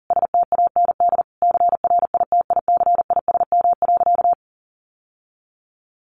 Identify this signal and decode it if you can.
Morse code, heard as a keyed tone, about 40 wpm, 700 Hz. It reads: STAND CRITICISM.
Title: Sentences from Top 1300 Words 40wpm